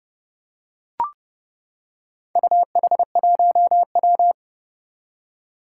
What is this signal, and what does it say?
Morse code, heard as a keyed tone, about 30 wpm, 700 Hz. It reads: UH1W
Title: Four Challenging Letter-Number Combinations 30wpm